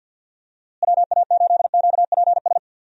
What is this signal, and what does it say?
Morse code, heard as a keyed tone, about 52 wpm, 700 Hz. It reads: WA8QPS
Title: USA Call Signs - 6 Characters - Repeated 3x - 50wpm